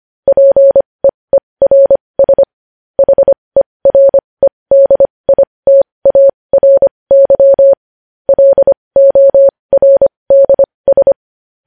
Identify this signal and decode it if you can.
Morse code, heard as a keyed tone, about 25 wpm, spaced as for 20 wpm, 550 Hz. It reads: PEERS HEREDITARY LORDS
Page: news-headlines-in-morse-code-at-20-wpm